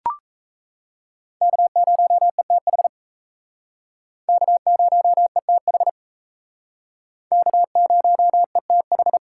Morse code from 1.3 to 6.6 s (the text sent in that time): K0ETH K0ETH